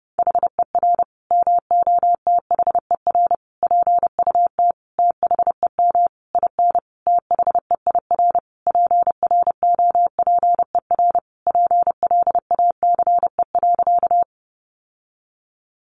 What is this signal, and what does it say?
Morse code, heard as a keyed tone, about 30 wpm, 700 Hz. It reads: HER MOTHER PUT THEM IN THEIR PROPER PLACE.
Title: Sentences from Top 800 Words 30wpm